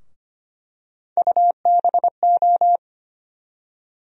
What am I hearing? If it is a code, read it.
Morse code, heard as a keyed tone, about 25 wpm, 700 Hz. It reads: UBO
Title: USA Call Sign Postfixes - 3 Letters 25wpm